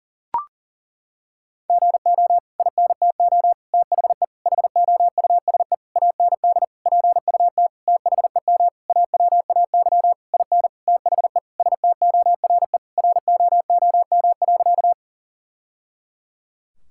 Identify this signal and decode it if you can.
Morse code, heard as a keyed tone, about 37 wpm, 700 Hz. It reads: GO INTO THE HOUSE AND PUT THEM AWAY IN THE STORE ROOM.
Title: Sentences from Top 1000 Words - Review Part 5 - 40wpm